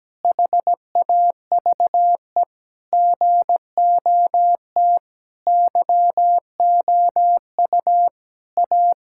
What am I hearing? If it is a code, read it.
Morse code, heard as a keyed tone, about 17 wpm, 700 Hz. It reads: HAVE GOT YOU A
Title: Sentences from Top 1000 Words - Review Part 5 - 17wpm